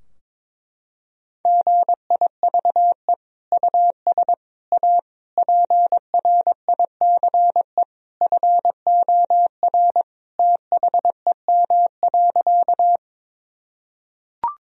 Morse code, heard as a keyed tone, about 22 wpm, 700 Hz. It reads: GIVE US A PRICE FOR THEM.
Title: Sentences from Top 1000 Words - Review Part 4 - 22wpm